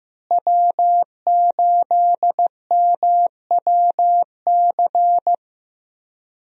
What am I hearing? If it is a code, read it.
Morse code, heard as a keyed tone, about 16 wpm, 700 Hz. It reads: W8MWC
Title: USA Call Signs - 5 Characters - Repeated 2x - 15wpm